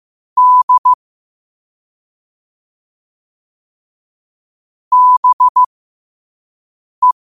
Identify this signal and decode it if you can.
Morse code, heard as a keyed tone, about 15 wpm, spaced as for 5 wpm, 1000 Hz. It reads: D BE
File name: qotdfiles / 5wpm